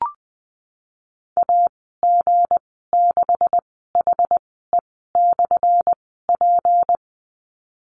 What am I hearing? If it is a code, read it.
Morse code, heard as a keyed tone, about 20 wpm, spaced as for 15 wpm, 700 Hz. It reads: AG6HE/P
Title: 179 Lesson 29g - Slash - Call Signs 20wpm